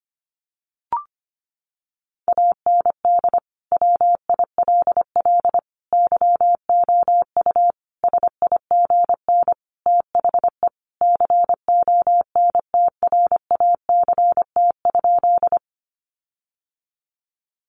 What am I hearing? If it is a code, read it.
Morse code, heard as a keyed tone, about 25 wpm, 700 Hz. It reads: AND WILL YOU SIGN THE CONTRACT?
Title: Sentences from Top 1400 Words 25wpm